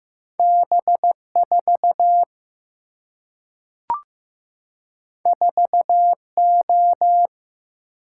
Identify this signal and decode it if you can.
Morse code, heard as a keyed tone, about 15 wpm, 700 Hz. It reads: B4 4O
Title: Single-Letter Single-Number 15wpm